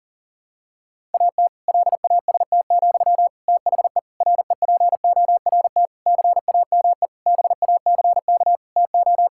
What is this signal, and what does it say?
Morse code, heard as a keyed tone, about 40 wpm, 700 Hz. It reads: AT LAST, THE REPORT CAME BACK TO
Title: Sentences from Top 1000 Words - Review Part 6 - 40wpm